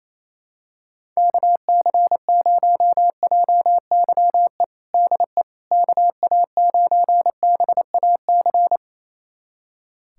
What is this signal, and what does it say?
Morse code, heard as a keyed tone, about 28 wpm, 700 Hz. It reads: KC0JYE DE KA9BAC
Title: USA Callsign DE Callsign - 6 Characters - 28wpm